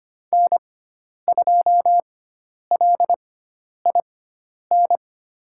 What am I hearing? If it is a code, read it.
Morse code, heard as a keyed tone, about 25 wpm, spaced as for 10 wpm, 700 Hz. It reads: N2LIN
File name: Lesson 16g - Letter c - Call Signs 25wpm